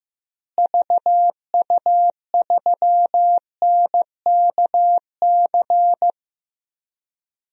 Morse code, heard as a keyed tone, about 15 wpm, 700 Hz. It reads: VU3NKC
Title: DXCC-100 Call Signs - 6 Characters - Repeated 3x - 15wpm